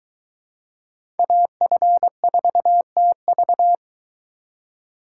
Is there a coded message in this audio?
Yes: Morse code from 1.2 to 3.8 s, heard as a keyed tone, about 22 wpm, 700 Hz. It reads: AF4TV